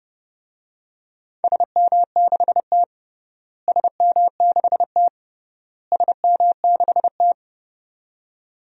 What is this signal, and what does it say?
Morse code, heard as a keyed tone, about 30 wpm, 700 Hz. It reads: SM6T SM6T SM6T
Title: DXCC-100 Call Signs - 4 Characters - Repeated 3x - 30wpm